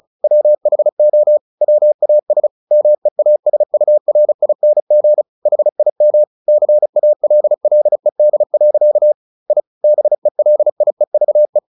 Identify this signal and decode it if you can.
Morse code, heard as a keyed tone, about 34 wpm, 600 Hz. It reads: WHO WAS MEASURING HIM CALLED. I BELIEVE
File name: the old man and the sea 2 35wpm